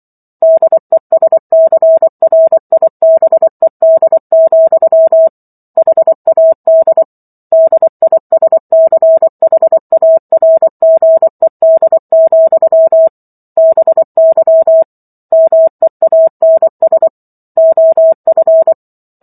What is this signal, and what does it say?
Morse code, heard as a keyed tone, about 24 wpm, 650 Hz. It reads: DESCRIBED, HAD DISCHARGED, BY MEANS OF